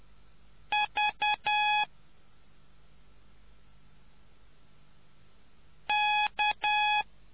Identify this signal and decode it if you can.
Morse code, heard as a keyed tone, about 10 wpm, 840 Hz. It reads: V K